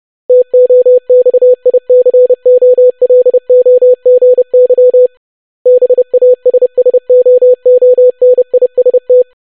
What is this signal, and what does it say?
Morse code, heard as a keyed tone, about 30 wpm, 500 Hz. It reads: TOXICOLOGY BASSOONIST
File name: longwords-25wpm